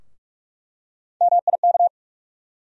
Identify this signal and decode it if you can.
Morse code, heard as a keyed tone, about 47 wpm, 700 Hz. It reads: MIK
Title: USA Call Sign Postfixes - 3 Letters 45wpm